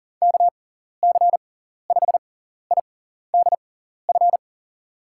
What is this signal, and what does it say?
Morse code, heard as a keyed tone, about 40 wpm, spaced as for 14 wpm, 700 Hz. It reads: KC5IDF
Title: USA Call Signs - 6x Character Spacing - 40wpm